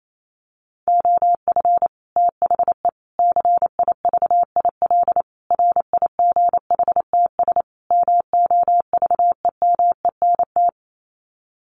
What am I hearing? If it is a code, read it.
Morse code, heard as a keyed tone, about 28 wpm, 700 Hz. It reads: OF THE CIVIL RIGHTS MOVEMENT
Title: Sets of 5 Words Encore2 28wpm